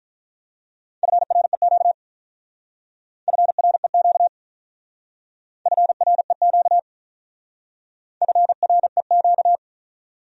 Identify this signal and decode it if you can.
Morse code, heard as a keyed tone, about 43 wpm, 700 Hz. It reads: FREQ FREQ FREQ FREQ
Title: Speed-Racing Abbreviations - 35wpm